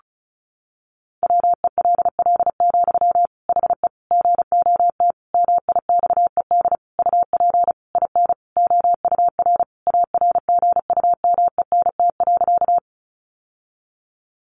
Morse code, heard as a keyed tone, about 34 wpm, 700 Hz. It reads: WELL, HE GOT MIXED UP IN OUR ARGUMENT.
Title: Sentences from Top 1500 Words 35wpm